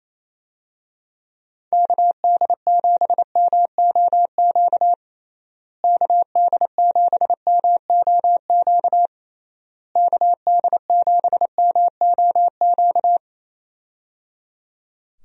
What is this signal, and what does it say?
Morse code, heard as a keyed tone, about 28 wpm, 700 Hz. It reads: KD7MOQ KD7MOQ KD7MOQ
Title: USA Call Signs - 6 Characters - Repeated 3x - 28wpm